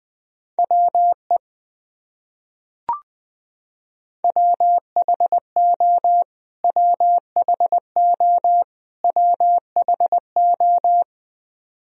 Morse code, heard as a keyed tone, about 21 wpm, 700 Hz. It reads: WE WHO WHO WHO
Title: Top 100 Words Repeated 20wpm